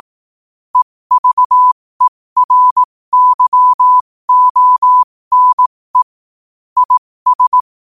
Morse code, heard as a keyed tone, about 18 wpm, spaced as for 15 wpm, 1000 Hz. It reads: EVERYONE IS
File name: qotdfiles / 15wpm